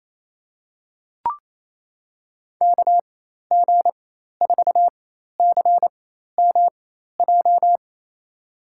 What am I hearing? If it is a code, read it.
Morse code, heard as a keyed tone, about 28 wpm, spaced as for 13 wpm, 690 Hz. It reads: KG4CMJ
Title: USA Call Signs - 6 Characters - 4x Character Spacing - 28wpm